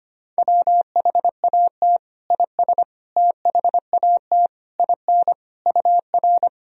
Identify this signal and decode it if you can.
Morse code, heard as a keyed tone, about 25 wpm, 700 Hz. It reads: WHAT IS THAT IN UR
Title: Sentences from Top 200 Words and Abbreviations 25wpm